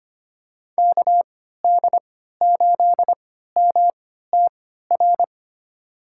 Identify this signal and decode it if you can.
Morse code, heard as a keyed tone, about 25 wpm, spaced as for 14 wpm, 700 Hz. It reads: KD8MTR
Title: USA Call Signs - 6 Characters - 3x Character Spacing - 25wpm